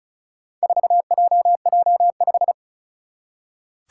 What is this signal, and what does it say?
Morse code, heard as a keyed tone, about 36 wpm, 700 Hz. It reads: 4JJ5
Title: Four Challenging Letter-Number Combinations 35wpm